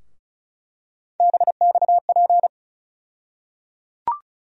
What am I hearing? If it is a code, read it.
Morse code, heard as a keyed tone, about 36 wpm, 700 Hz. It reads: BXP
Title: USA Call Sign Postfixes - 3 Letters 35wpm